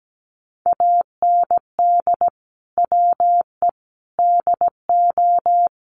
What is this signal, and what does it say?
Morse code, heard as a keyed tone, about 17 wpm, 700 Hz. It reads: AND WE DO
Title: Sets of 5 Words Encore 17wpm